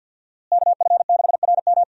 Morse code, heard as a keyed tone, about 47 wpm, 700 Hz. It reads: KF6RM
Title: USA Call Signs - 5 Characters - Repeated 3x - 50wpm